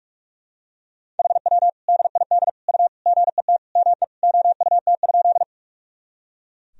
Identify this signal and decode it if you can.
Morse code, heard as a keyed tone, about 44 wpm, 700 Hz. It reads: HW DID U GET ME OUT?
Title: Sentences from Top 200 Words and Abbreviations 45wpm